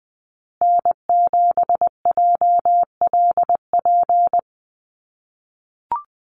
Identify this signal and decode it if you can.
Morse code, heard as a keyed tone, about 19 wpm, 700 Hz. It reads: N7JLP
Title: USA Call Signs Repeated 2x 20wpm